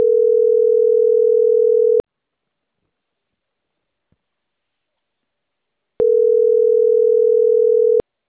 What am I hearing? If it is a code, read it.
Ringback tone (North America), 440 + 480 Hz.